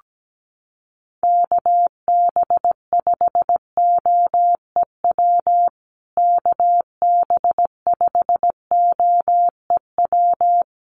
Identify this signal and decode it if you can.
Morse code, heard as a keyed tone, about 17 wpm, 700 Hz. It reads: KB5OEW KB5OEW
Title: USA Call Signs Repeated 3x 17wpm